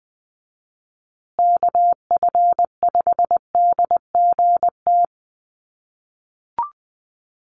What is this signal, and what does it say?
Morse code, heard as a keyed tone, about 20 wpm, 700 Hz. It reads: KF5DGT
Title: USA Call Signs Repeated 3x 20wpm